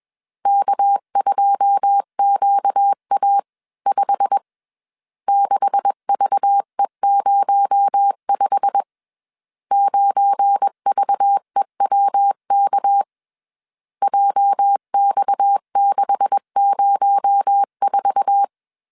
Morse code, heard as a keyed tone, about 21 wpm, 800 Hz. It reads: K2QA5 6VE05 9VEWK JX604